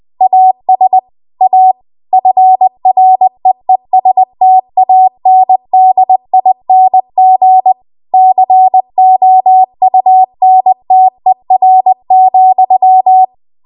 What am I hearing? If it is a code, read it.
Morse code, heard as a keyed tone, about 20 wpm, 750 Hz. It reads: AS A FREESTANDING COUNTER,
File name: Archive / 20 WPM / 20WPM